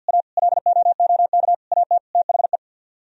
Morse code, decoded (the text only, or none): A LOOK AT THE